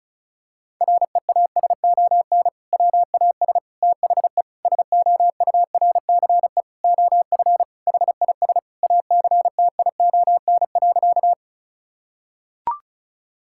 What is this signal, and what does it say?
Morse code, heard as a keyed tone, about 34 wpm, 700 Hz. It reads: REASON WAS THE SOURCE OF HIS ACTION.
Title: Sentences from Top 1000 Words - Review Part 5 - 35wpm